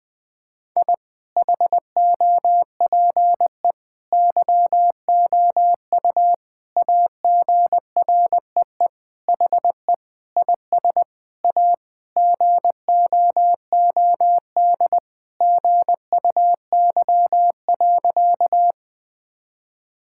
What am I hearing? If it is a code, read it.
Morse code, heard as a keyed tone, about 20 wpm, 700 Hz. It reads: I HOPE YOU AGREE HE IS A GOOD GUY.
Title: Sentences with 8-Word Phrases - 20wpm